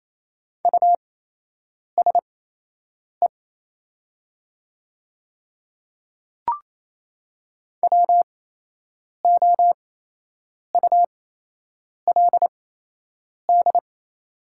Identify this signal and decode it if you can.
Morse code, heard as a keyed tone, about 27 wpm, spaced as for 8 wpm, 700 Hz. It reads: USE WOULD
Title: 8x Character Spacing - Top 100 Words - 28wpm